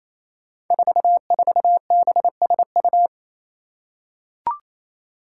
Morse code, heard as a keyed tone, about 28 wpm, 700 Hz. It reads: 44BSU
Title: Five Challenging Letter-Number Combinations 28wpm